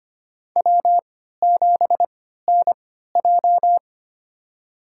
Morse code, heard as a keyed tone, about 25 wpm, spaced as for 14 wpm, 700 Hz. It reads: W7NJ
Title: USA Call Signs - 3x Character Spacing - 25wpm